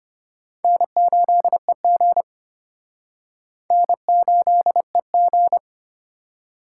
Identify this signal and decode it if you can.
Morse code, heard as a keyed tone, about 31 wpm, 700 Hz. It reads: N8EG N8EG
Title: Speed Racing - USA 1x2 Call Signs 25wpm